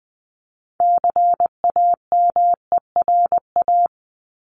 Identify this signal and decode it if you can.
Morse code, heard as a keyed tone, about 20 wpm, 700 Hz. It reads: CAMERA
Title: Top 501-750 Words 20wpm